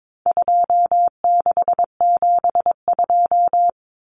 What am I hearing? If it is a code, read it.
Morse code, heard as a keyed tone, about 22 wpm, 700 Hz. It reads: 2672
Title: Numbers - 4-Digit 22wpm